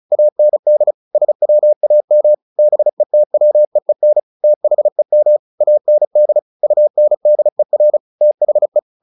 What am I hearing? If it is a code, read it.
Morse code, heard as a keyed tone, about 34 wpm, 600 Hz. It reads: AND SWAM BETWEEN THEM AND UNDER THE
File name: the old man and the sea / the old man and the sea 1 35wpm